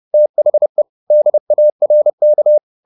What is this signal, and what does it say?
Morse code, heard as a keyed tone, about 30 wpm, 600 Hz. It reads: THE DARK
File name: the old man and the sea / the old man and the sea 1 30wpm